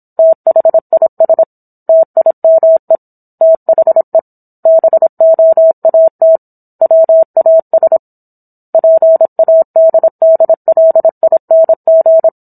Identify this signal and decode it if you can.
Morse code, heard as a keyed tone, about 27 wpm, 650 Hz. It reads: THIS TIME THE BOAT WAS PADDLING